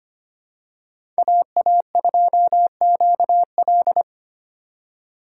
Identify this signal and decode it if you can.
Morse code, heard as a keyed tone, about 25 wpm, 700 Hz. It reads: AA2QL
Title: USA Call Signs - 5 Characters - 3x Character Spacing - 25wpm